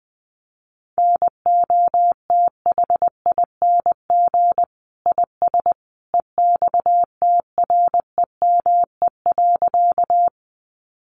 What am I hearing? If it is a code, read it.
Morse code, heard as a keyed tone, about 20 wpm, 700 Hz. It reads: NOTHING IS EXTREME.
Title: Sentences from Top 1300 Words 20wpm